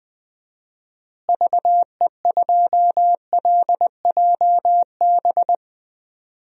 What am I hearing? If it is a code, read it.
Morse code, heard as a keyed tone, about 20 wpm, 700 Hz. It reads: VE2LJB
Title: DXCC-100 Call Signs - 6 Characters - Repeated 2x - 20wpm